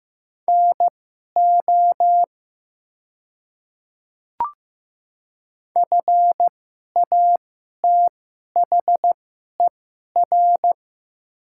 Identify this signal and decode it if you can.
Morse code, heard as a keyed tone, about 15 wpm, spaced as for 11 wpm, 700 Hz. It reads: NO FATHER
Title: 2x Character Spacing - Top 200 Words - 15wpm